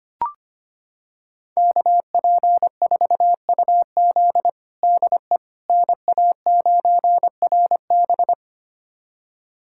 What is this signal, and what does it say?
Morse code, heard as a keyed tone, about 25 wpm, 700 Hz. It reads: KP4UZ DE NA9RB
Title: USA Callsign DE Callsign - 5 Characters - 25wpm